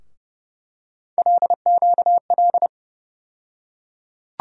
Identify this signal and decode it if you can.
Morse code, heard as a keyed tone, about 31 wpm, 700 Hz. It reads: LQL
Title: USA Call Sign Postfixes - 3 Letters 30wpm